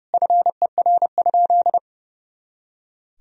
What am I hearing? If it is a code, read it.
Morse code, heard as a keyed tone, about 30 wpm, 700 Hz. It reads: FER?
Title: Sentences from Top 100 Words and Abbreviations 30wpm